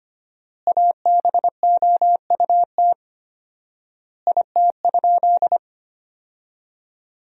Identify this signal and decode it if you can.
Morse code, heard as a keyed tone, about 25 wpm, 700 Hz. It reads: ABOUT IT?
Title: 4x Word-Spacing - Sentences from Top 200 Words - 25wpm